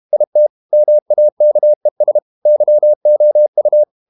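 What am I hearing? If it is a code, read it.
Morse code, heard as a keyed tone, about 32 wpm, 600 Hz. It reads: IT MAKES YOU
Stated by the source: Morse code practice at 32 wpm from CW med Gnister, reading Ken Follett's novel Never in English.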